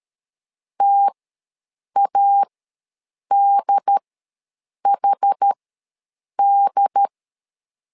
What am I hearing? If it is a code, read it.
Morse code, heard as a keyed tone, about 13 wpm, spaced as for 7 wpm, 800 Hz. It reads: TADHD
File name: LEZIONI / 2 UHDN / 2 UHD ws 12wpm 1722